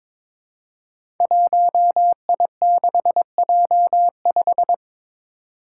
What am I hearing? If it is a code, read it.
Morse code, heard as a keyed tone, about 22 wpm, 700 Hz. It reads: 1I6J5
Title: Five Challenging Letter-Number Combinations 22wpm